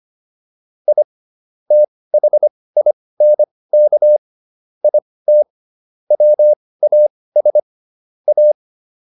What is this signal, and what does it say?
Morse code, heard as a keyed tone, about 25 wpm, spaced as for 18 wpm, 600 Hz. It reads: I THINK IT WAS A